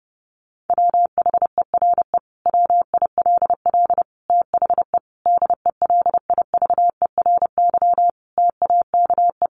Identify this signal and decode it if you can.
Morse code, heard as a keyed tone, about 30 wpm, 700 Hz. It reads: WHERE WILL THE DELIVERY TAKE
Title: Sentences from Top 1500 Words 30wpm